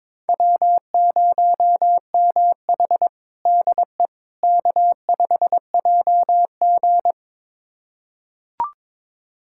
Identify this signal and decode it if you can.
Morse code, heard as a keyed tone, about 22 wpm, 700 Hz. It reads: W0MH DE K5JG
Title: USA Callsign DE Callsign - 4 Characters - 22wpm